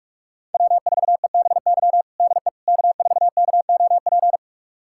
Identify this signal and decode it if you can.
Morse code, heard as a keyed tone, about 47 wpm, 700 Hz. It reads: W4EBY DE K4KOP
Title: USA Callsign DE Callsign - 5 Characters - 45wpm